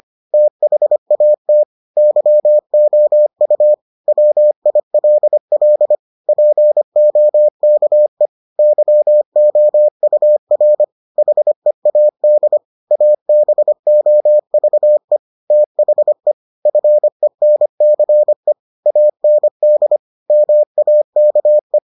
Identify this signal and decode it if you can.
Morse code, heard as a keyed tone, about 25 wpm, 600 Hz. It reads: THAT YOU WILL POKE YOUR HEAD ABOVE THE FENCE AND MAKE